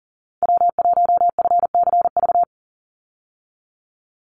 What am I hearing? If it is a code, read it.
Morse code, heard as a keyed tone, about 40 wpm, 700 Hz. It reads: W1FCV
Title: Call Signs - 40wpm